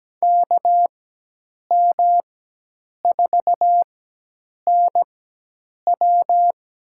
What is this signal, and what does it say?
Morse code, heard as a keyed tone, about 17 wpm, spaced as for 8 wpm, 700 Hz. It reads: KM4NW
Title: USA Call Signs - 5 Characters - 4x Character Spacing - 17wpm